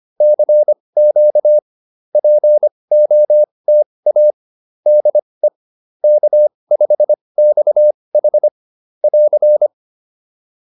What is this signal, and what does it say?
Morse code, heard as a keyed tone, about 25 wpm, spaced as for 20 wpm, 600 Hz. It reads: CQ POTA DE K5XH <AR>